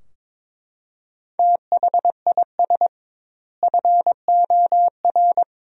Morse code, heard as a keyed tone, about 22 wpm, 700 Hz. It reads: THIS FOR
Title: 2x Word-Spacing - Sentences from Top 200 Words - 22wpm